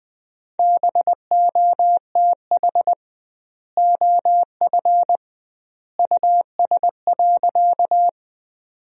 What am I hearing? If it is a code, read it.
Morse code, heard as a keyed tone, about 20 wpm, 700 Hz. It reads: BOTH OF US.
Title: 2x Word-Spacing - Sentences from Top 300 Words - 20wpm